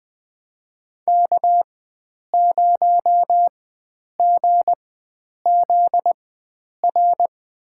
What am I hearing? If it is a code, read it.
Morse code, heard as a keyed tone, about 20 wpm, spaced as for 9 wpm, 700 Hz. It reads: K0GZR
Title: USA Call Signs - 5 Characters - 4x Character Spacing - 20wpm